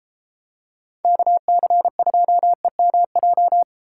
Morse code, heard as a keyed tone, about 31 wpm, 700 Hz. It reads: KC2EMJ